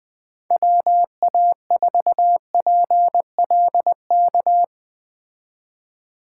Morse code, heard as a keyed tone, about 19 wpm, 700 Hz. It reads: WA4PLK